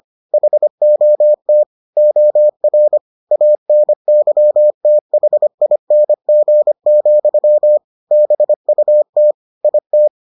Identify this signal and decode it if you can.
Morse code, heard as a keyed tone, about 25 wpm, 600 Hz. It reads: HOT OR ANYTHING, BUT IT